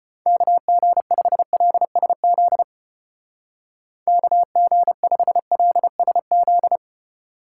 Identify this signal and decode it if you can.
Morse code, heard as a keyed tone, about 31 wpm, 700 Hz. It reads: KG5LSZ KG5LSZ
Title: Speed Racing - USA 2x3 Call Signs 25wpm